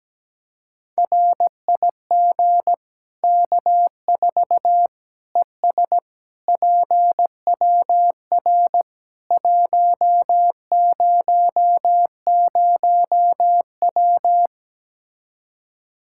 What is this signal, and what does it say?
Morse code, heard as a keyed tone, about 17 wpm, 700 Hz. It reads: RIG K4 ES PWR 100W
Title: Practice QSO - Exchange 2 - Info - 17wpm